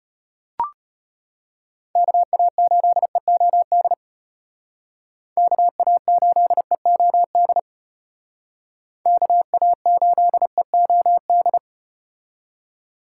Morse code, heard as a keyed tone, about 35 wpm, 700 Hz. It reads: KA8EOD KA8EOD KA8EOD